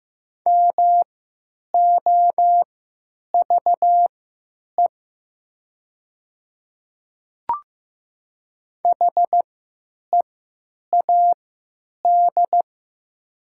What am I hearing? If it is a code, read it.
Morse code, heard as a keyed tone, about 15 wpm, spaced as for 8 wpm, 700 Hz. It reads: MOVE HEAD
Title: 3x Character Spacing - Top 300 Words - 15wpm